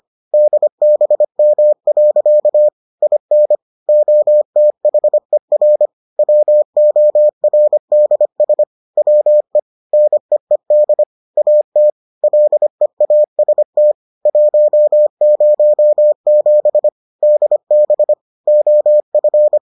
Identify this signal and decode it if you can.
Morse code, heard as a keyed tone, about 25 wpm, 600 Hz. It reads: DBM. IN OTHER WORDS WE NEED AT LEAST 107 DB OF